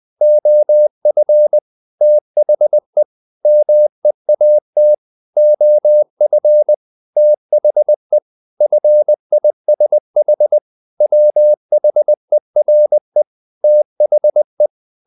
Morse code, heard as a keyed tone, about 20 wpm, 600 Hz. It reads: OF THE MEAT OF THE FISH WHERE THE
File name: the old man and the sea part 2 20wpm